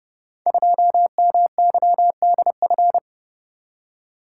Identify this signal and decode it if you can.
Morse code, heard as a keyed tone, about 30 wpm, 700 Hz. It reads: 2MYDF
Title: Code Groups - 5 Letter:Numbers - 30wpm